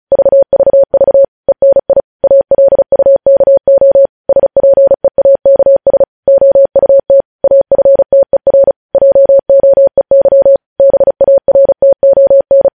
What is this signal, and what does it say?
Morse code, heard as a keyed tone, about 36 wpm, 550 Hz. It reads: VVV ENI ALUKO SPEAKS OUT AFTER JOEY BARTON
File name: podcast / News headlines-1x-35x35